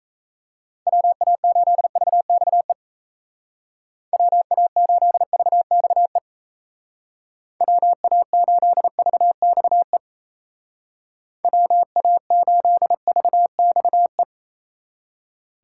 Morse code, heard as a keyed tone, about 36 wpm, 700 Hz. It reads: WA8VXE WA8VXE WA8VXE WA8VXE